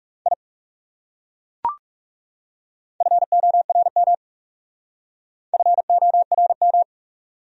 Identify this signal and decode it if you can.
Morse code, heard as a keyed tone, about 36 wpm, 700 Hz. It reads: I FORM FORM
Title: Speed-Racing Top 200 Words 30wpm